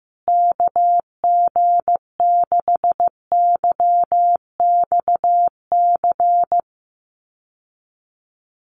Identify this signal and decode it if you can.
Morse code, heard as a keyed tone, about 15 wpm, 700 Hz. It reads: KG6YXC
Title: USA Call Signs Repeated 2x 15wpm